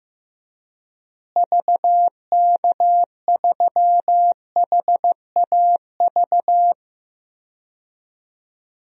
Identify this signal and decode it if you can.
Morse code, heard as a keyed tone, about 15 wpm, 700 Hz. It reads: VK3HAV
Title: DXCC-100 Call Signs - 6 Characters - Repeated 2x - 15wpm